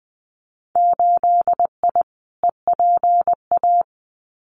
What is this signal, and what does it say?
Morse code, heard as a keyed tone, about 20 wpm, 700 Hz. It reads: 8I EPA
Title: Winter Field Day Exchanges Encore 20wpm